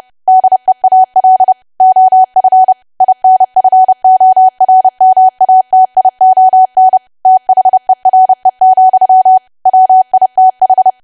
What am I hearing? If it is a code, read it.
Morse code, heard as a keyed tone, about 30 wpm, 750 Hz. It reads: DEAL OF INFORMATION THERE, WITH
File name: cwa / W1AW-Level3 / 30WPM